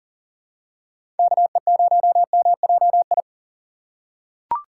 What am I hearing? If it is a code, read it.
Morse code, heard as a keyed tone, about 40 wpm, 700 Hz. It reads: KE0MJI
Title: USA Call Signs - 6 Characters - 4x Character Spacing - 40wpm